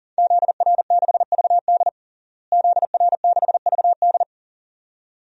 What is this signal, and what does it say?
Morse code, heard as a keyed tone, about 40 wpm, 700 Hz. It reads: ZR6VD ZR6VD
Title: DXCC-100 Call Signs - 5 Characters - Repeated 2x - 40wpm